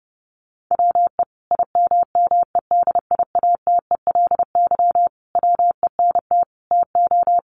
Morse code, heard as a keyed tone, about 30 wpm, 700 Hz. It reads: WE IMMEDIATELY WENT TO
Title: Sentences from Top 1100 Words 30wpm